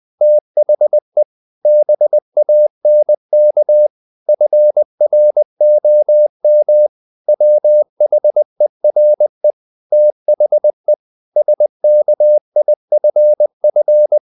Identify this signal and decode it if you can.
Morse code, heard as a keyed tone, about 20 wpm, 600 Hz. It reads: THE BANK FROM WHERE THE SKIFF
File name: the old man and the sea / the old man and the sea part 1 20wpm